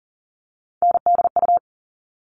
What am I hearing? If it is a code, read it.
Morse code, heard as a keyed tone, about 40 wpm, 700 Hz. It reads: NDU
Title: All Cut Numbers - 3-Digit 40wpm